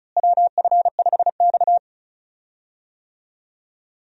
Morse code, heard as a keyed tone, about 36 wpm, 700 Hz. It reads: WF5X
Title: USA Call Signs - 4 Characters - Repeated 3x - 35wpm